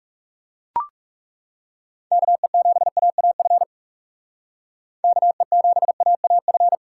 Morse code, heard as a keyed tone, about 37 wpm, 700 Hz. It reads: KE7AAF KE7AAF
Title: Speed Racing - USA 2x3 Call Signs 30wpm